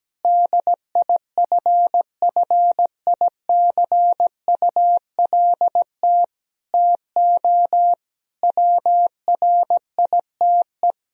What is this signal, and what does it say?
Morse code, heard as a keyed tone, about 17 wpm, 700 Hz. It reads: DIFFICULT TO WRITE
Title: Sentences from Top 1000 Words - Review Part 3 - 17wpm